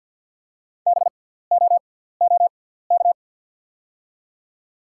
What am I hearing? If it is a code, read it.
Morse code, heard as a keyed tone, about 47 wpm, spaced as for 17 wpm, 690 Hz. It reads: BOOK
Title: 6x Character Spacing - Top 500 Words - 50wpm